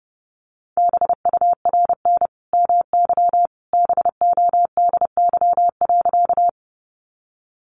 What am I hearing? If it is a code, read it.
Morse code, heard as a keyed tone, about 30 wpm, 700 Hz. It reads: BURN MY BODY.
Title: Sentences from Top 1300 Words 30wpm